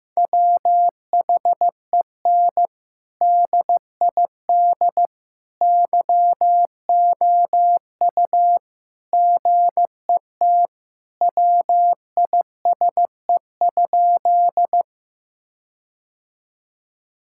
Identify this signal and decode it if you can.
Morse code, heard as a keyed tone, about 15 wpm, 700 Hz. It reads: WHEN DID YOU GET WISE?
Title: Sentences from Top 1600 Words 15wpm